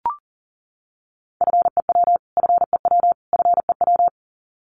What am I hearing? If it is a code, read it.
Morse code, heard as a keyed tone, about 40 wpm, 700 Hz. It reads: FEW FEW FEW
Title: Top 500 Words Repeated 40wpm